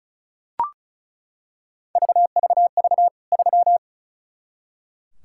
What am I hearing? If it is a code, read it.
Morse code, heard as a keyed tone, about 36 wpm, 700 Hz. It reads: VVV 3
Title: K1USN SST QSO Practice 35wpm